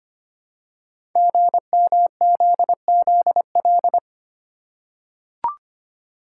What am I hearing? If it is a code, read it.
Morse code, heard as a keyed tone, about 25 wpm, 700 Hz. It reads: GMZZL
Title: Code Groups - 5 Letters - 25wpm